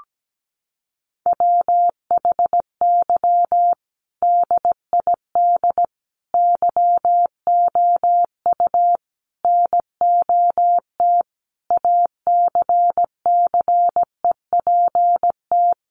Morse code, heard as a keyed tone, about 17 wpm, 700 Hz. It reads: WHY DID YOU NOT ACCEPT
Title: Sentences from Top 1300 Words 17wpm